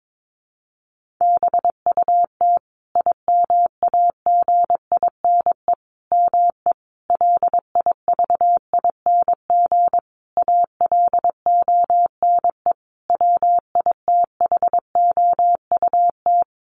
Morse code, heard as a keyed tone, about 22 wpm, 700 Hz. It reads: BUT IMAGINE ME LIVING ALONE WITHOUT
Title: Sentences from Top 1400 Words 22wpm